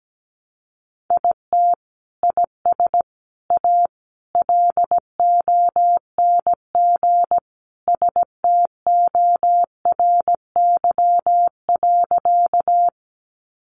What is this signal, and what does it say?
Morse code, heard as a keyed tone, about 17 wpm, 700 Hz. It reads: IT IS A LONG STORY.
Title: Sentences from Top 300 Words 17wpm